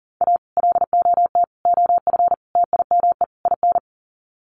Morse code, heard as a keyed tone, about 37 wpm, 700 Hz. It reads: A LOT OF TIME IN
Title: Sets of 5 Words Encore2 40wpm